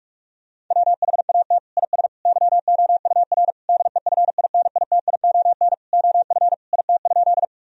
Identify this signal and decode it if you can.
Morse code, heard as a keyed tone, about 46 wpm, 700 Hz. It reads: WHAT IS YOUR DEFINITION OF IT?
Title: Sentences from Top 1000 Words - Review Part 4 - 45wpm